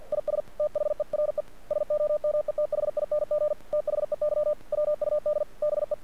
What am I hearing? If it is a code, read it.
Morse code, heard as a keyed tone, about 48 wpm, 600 Hz. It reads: IS THERE SOMETHING THEY WUD BE
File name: archived / CRSnet